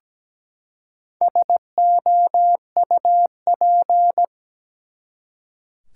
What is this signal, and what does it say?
Morse code, heard as a keyed tone, about 17 wpm, 700 Hz. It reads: SOUP
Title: Four Letter Words - 17wpm